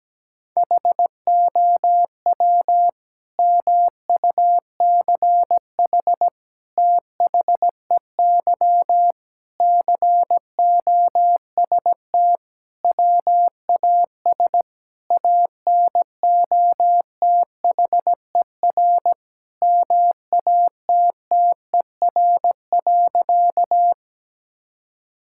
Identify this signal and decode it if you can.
Morse code, heard as a keyed tone, about 17 wpm, 700 Hz. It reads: HOW MUCH THEY COST WAS ANOTHER MATTER.
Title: Sentences from Top 1000 Words - Review Part 5 - 17wpm